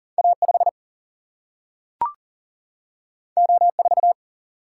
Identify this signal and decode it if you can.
Morse code, heard as a keyed tone, about 38 wpm, 700 Hz. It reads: A5 O4
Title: Single-Letter Single-Number 40wpm